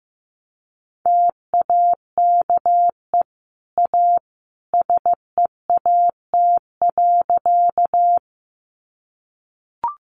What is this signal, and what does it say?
Morse code, heard as a keyed tone, about 15 wpm, 700 Hz. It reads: TAKE A SEAT.
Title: Sentences from Top 1500 Words 15wpm